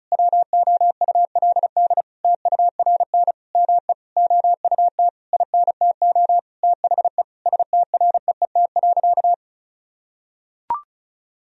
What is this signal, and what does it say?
Morse code, heard as a keyed tone, about 36 wpm, 700 Hz. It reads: WOULD TURN ME OUT INTO THE STREET.
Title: Sentences from Top 1000 Words - Review Part 4 - 35wpm